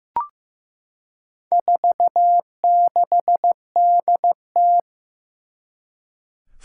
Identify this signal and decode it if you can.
Morse code, heard as a keyed tone, about 15 wpm, 700 Hz. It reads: 46DT
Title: Four Challenging Letter-Number Combinations 15wpm